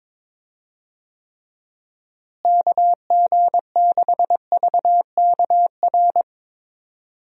Morse code, heard as a keyed tone, about 22 wpm, 700 Hz. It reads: KG6VKR
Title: USA Call Signs - 6 Characters - Repeated 3x - 22wpm